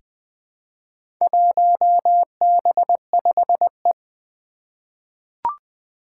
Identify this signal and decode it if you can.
Morse code, heard as a keyed tone, about 20 wpm, 700 Hz. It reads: 1B5E
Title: Four Challenging Letter-Number Combinations 20wpm